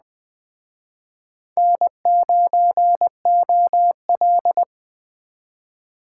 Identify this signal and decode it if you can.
Morse code, heard as a keyed tone, about 21 wpm, 700 Hz. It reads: N9OL
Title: Speed Racing - USA 1x2 Call Signs 17wpm